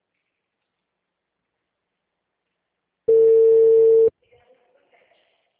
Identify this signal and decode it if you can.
Ringback tone (China, 450 Hz), with speech.